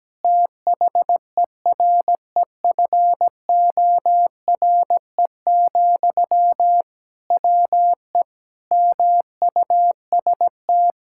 Morse code, heard as a keyed tone, about 17 wpm, 700 Hz. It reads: THEREFORE, WE MUST
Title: Sentences from Top 1000 Words - Review Part 5 - 17wpm